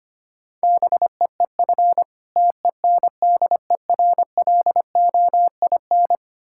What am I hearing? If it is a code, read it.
Morse code, heard as a keyed tone, about 25 wpm, 700 Hz. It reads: BEEF TENDERLOIN
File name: practice / 25wpm / Difficult Words with Hints - 25wpm